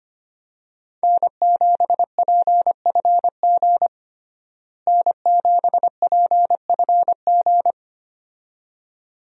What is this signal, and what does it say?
Morse code, heard as a keyed tone, about 25 wpm, 700 Hz. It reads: N7PFG N7PFG
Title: USA Call Signs - 5 Characters - Repeated 2x - 25wpm